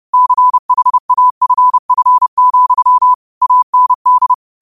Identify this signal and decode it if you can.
Morse code, heard as a keyed tone, about 30 wpm, 1000 Hz. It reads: CHAFF, AND